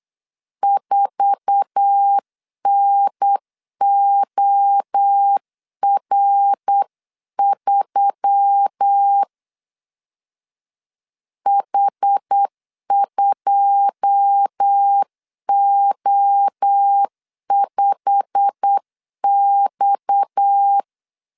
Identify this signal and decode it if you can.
Morse code, heard as a keyed tone, about 8 wpm, 800 Hz. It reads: 4NOR3 H2O5X
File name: CW / DET / 1 9 1500C S A 8wpm 5637